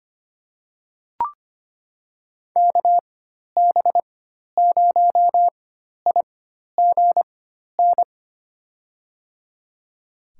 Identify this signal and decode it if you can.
Morse code, heard as a keyed tone, about 25 wpm, spaced as for 12 wpm, 700 Hz. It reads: KB0IGN
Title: USA Call Signs - 4x Character Spacing - 25wpm